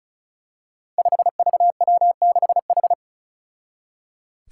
Morse code, heard as a keyed tone, about 36 wpm, 700 Hz. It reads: 5VW6H
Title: Five Challenging Letter-Number Combinations 35wpm